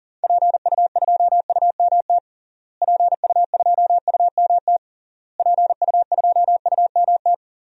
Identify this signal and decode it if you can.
Morse code, heard as a keyed tone, about 40 wpm, 700 Hz. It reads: PU2UMT PU2UMT PU2UMT
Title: DXCC-100 Call Signs - 6 Characters - Repeated 3x - 40wpm